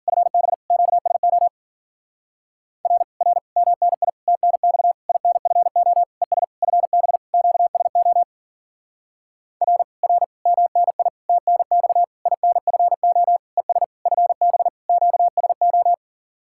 Morse code, heard as a keyed tone, about 52 wpm, 700 Hz. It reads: FB QSO R R MNI TNX INFO ES FB QSO R R MNI TNX INFO ES FB QSO